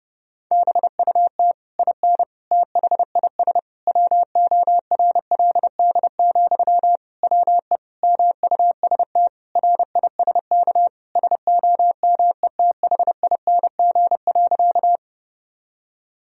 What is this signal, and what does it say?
Morse code, heard as a keyed tone, about 30 wpm, 700 Hz. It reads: BUT IN THIS WORLD, WE MUST RISK SOMETHING.
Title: Sentences from Top 1000 Words - Review Part 4 - 30wpm